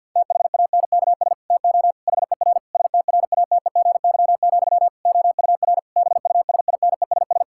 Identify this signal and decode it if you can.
Morse code, heard as a keyed tone, about 52 wpm, 700 Hz. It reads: THANKS TO HER STRATEGY, OUR BUSINESS
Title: Sentences from Top 1000 Words - Review Part 5 - 50wpm